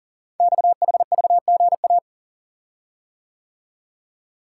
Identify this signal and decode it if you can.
Morse code, heard as a keyed tone, about 42 wpm, 690 Hz. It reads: XHVGA
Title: Code Groups - 5 Letters - 40wpm